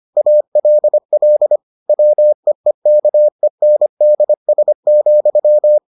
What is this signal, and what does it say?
Morse code, heard as a keyed tone, about 25 wpm, 600 Hz. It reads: ALL WEEKENDS,